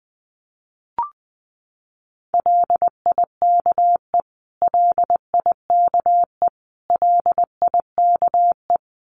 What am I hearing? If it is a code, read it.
Morse code, heard as a keyed tone, about 20 wpm, 700 Hz. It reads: LIKE LIKE LIKE
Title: Top 500 Words Repeated 20wpm